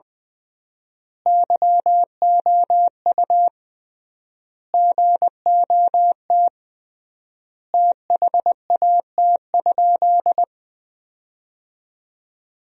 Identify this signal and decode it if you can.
Morse code, heard as a keyed tone, about 21 wpm, 700 Hz. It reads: YOU GOT THAT?
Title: 3x Word-Spacing - Sentences from Top 300 Words - 20wpm